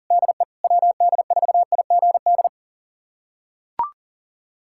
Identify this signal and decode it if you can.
Morse code, heard as a keyed tone, about 40 wpm, 700 Hz. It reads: DE WD4IGD
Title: USA Callsign DE Callsign 40wpm